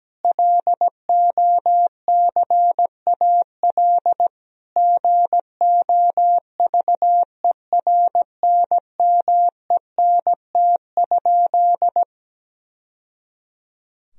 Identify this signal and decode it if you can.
Morse code, heard as a keyed tone, about 17 wpm, 700 Hz. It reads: LOCAL GOVERNMENT?
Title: Sentences from Top 1000 Words - Review Part 3 - 17wpm